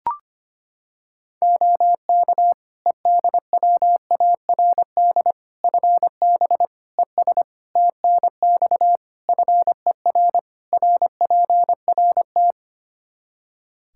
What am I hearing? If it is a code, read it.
Morse code, heard as a keyed tone, about 25 wpm, 700 Hz. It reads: OK EDWARD FB ES TNX FER RPRT
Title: Practice QSO - Exchange 2 - Info - 25wpm